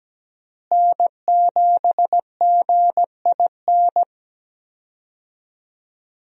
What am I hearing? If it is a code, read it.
Morse code, heard as a keyed tone, about 17 wpm, 700 Hz. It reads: N7GIN
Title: USA Call Signs - 5 Characters - Repeated 3x - 17wpm